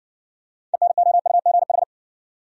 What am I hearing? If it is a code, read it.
Morse code, heard as a keyed tone, about 60 wpm, 700 Hz. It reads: ENOUGH